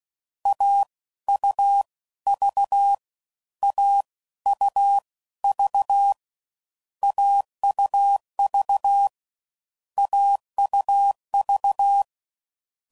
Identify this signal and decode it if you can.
Morse code, heard as a keyed tone, about 16 wpm, spaced as for 13 wpm, 800 Hz. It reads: AUV AUV AUV AUV
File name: Combinations / Difficult Combos / AUV Short